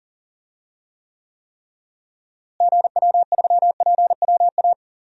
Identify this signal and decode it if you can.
Morse code, heard as a keyed tone, about 40 wpm, 700 Hz. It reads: GW3PWA
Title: DXCC-100 Call Signs - 6 Characters - Repeated 3x - 40wpm